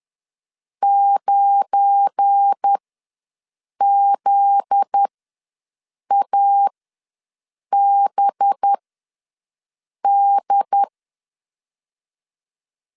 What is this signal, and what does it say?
Morse code, heard as a keyed tone, about 10 wpm, spaced as for 6 wpm, 800 Hz. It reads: 9ZABD
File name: LEZIONI / 7 GF9Z / 7 GF9Z ws 10wpm 3919